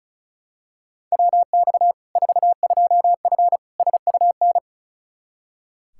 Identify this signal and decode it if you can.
Morse code, heard as a keyed tone, about 36 wpm, 700 Hz. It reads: WX 42F SUN
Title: K1USN SST QSO Practice 35wpm